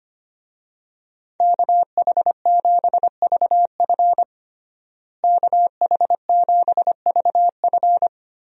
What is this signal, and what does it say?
Morse code, heard as a keyed tone, about 25 wpm, 700 Hz. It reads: KH7VF KH7VF
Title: USA Call Signs - 5 Characters - Repeated 2x - 25wpm